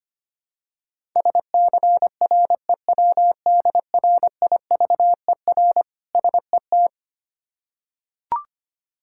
Morse code, heard as a keyed tone, about 25 wpm, 700 Hz. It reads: SCREWDRIVER SET
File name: practice / 25wpm / Difficult Words with Hints - 25wpm